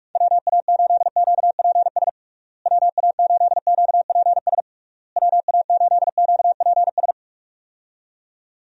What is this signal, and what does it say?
Morse code, heard as a keyed tone, about 46 wpm, 700 Hz. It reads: WA8QPS WA8QPS WA8QPS
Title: USA Call Signs - 6 Characters - Repeated 3x - 45wpm